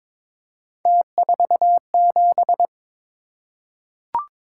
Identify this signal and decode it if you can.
Morse code, heard as a keyed tone, about 22 wpm, 700 Hz. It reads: T47
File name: practice / 22wpm / Three Letter-Number Combinations - 22wpm